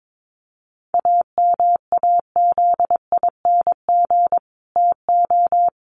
Morse code, heard as a keyed tone, about 22 wpm, 700 Hz. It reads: AMAZING TO
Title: Sentences from Top 1500 Words 22wpm